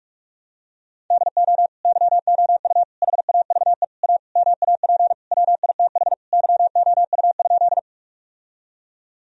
Morse code, heard as a keyed tone, about 46 wpm, 700 Hz. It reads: DO YOU HAVE A MAP WITH YOU?
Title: Sentences from Top 1000 Words - Review Part 2 - 45wpm